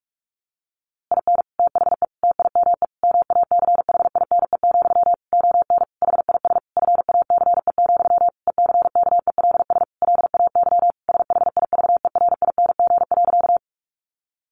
Morse code, heard as a keyed tone, about 44 wpm, 700 Hz. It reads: IN THE TIME MACHINE, ON HIS FACE, ECKELS LAY SHIVERING.
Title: A Sound Of Thunder 45wpm